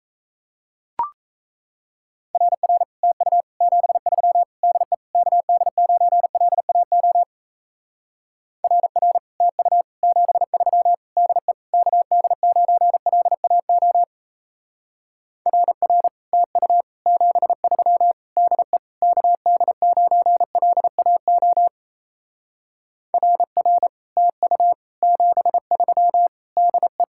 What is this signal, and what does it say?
Morse code, heard as a keyed tone, about 35 wpm, 700 Hz. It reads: RR TU 73 DE KD9LAO RR TU 73 DE KD9LAO RR TU 73 DE KD9LAO RR TU 73 DE